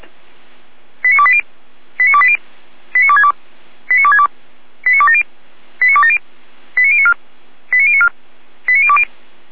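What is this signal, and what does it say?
Five-tone selective call (ZVEI), heard as quick runs of five tones, tones of about 70 ms. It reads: 89280 89280 89262 89262 89270 89270 89094 89094 89020
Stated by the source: Five-tone ZVEI selective call received by SDR.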